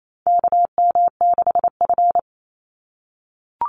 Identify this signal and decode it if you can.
Morse code, heard as a keyed tone, about 28 wpm, 700 Hz. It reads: KM6F
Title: USA Call Signs 28wpm